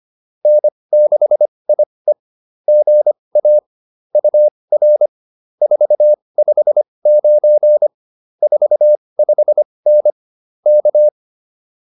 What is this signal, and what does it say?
Morse code, heard as a keyed tone, about 25 wpm, spaced as for 20 wpm, 590 Hz. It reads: N6IE GA UR 459 45N K